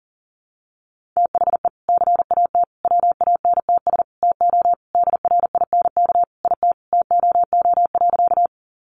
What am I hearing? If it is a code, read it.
Morse code, heard as a keyed tone, about 40 wpm, 700 Hz. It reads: THE CAT WANTS TO DRINK IT TOO.
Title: Additional Easy Phrases 40wpm